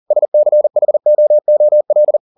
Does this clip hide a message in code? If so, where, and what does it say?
Yes: Morse code from 0.1 to 2.2 s, heard as a keyed tone, about 40 wpm, 600 Hz. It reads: SCHOOL